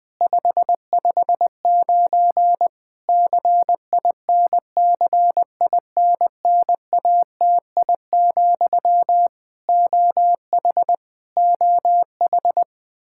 Morse code, heard as a keyed tone, about 20 wpm, 700 Hz. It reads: 559 CINCINNATI, OH OH
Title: Practice QSO - Exchange 1 - Beginning - 20wpm